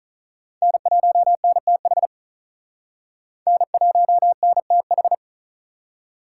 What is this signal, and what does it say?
Morse code, heard as a keyed tone, about 45 wpm, 700 Hz. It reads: N1NTH N1NTH